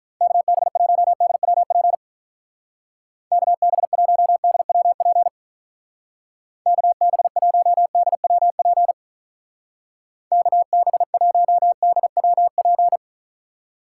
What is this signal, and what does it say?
Morse code, heard as a keyed tone, about 45 wpm, 700 Hz. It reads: KB1DWP KB1DWP KB1DWP KB1DWP